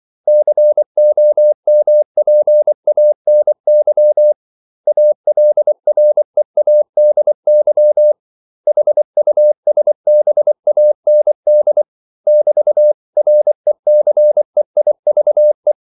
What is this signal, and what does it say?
Morse code, heard as a keyed tone, about 24 wpm, 600 Hz. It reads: COMPANY ALREADY HUSBAND <BT>RECEIVE
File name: CW / Practice / 500MostCommonEnglishWords24